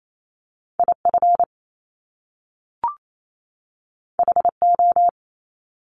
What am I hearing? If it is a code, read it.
Morse code, heard as a keyed tone, about 28 wpm, 700 Hz. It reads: IF HO
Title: Top Two Letter Combinations in Words 28wpm